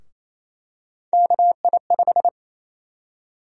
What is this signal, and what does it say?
Morse code, heard as a keyed tone, about 28 wpm, 700 Hz. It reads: KI5
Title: USA Call Sign Prefixes - 28wpm